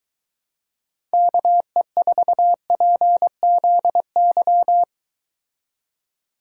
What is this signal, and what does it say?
Morse code, heard as a keyed tone, about 23 wpm, 700 Hz. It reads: KE4PZY